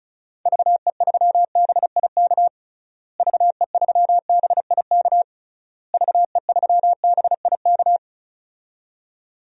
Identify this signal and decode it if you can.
Morse code, heard as a keyed tone, about 34 wpm, 700 Hz. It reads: VE3BIK VE3BIK VE3BIK
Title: DXCC-100 Call Signs - 6 Characters - Repeated 3x - 35wpm